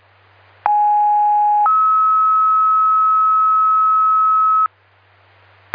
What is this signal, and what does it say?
Two-tone page, heard as a one-second tone, then a three-second tone, about 830 then 1290 Hz. It is heard from 0.7 to 4.7 s.